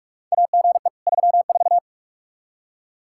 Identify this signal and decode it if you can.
Morse code, heard as a keyed tone, about 46 wpm, 700 Hz. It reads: AGE 34
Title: Practice QSO - Exchange 3 - More Info - 45wpm